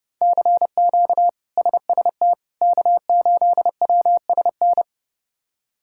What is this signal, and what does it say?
Morse code, heard as a keyed tone, about 30 wpm, 700 Hz. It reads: CQ SST K8WSN
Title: K1USN SST QSO Practice - 30wpm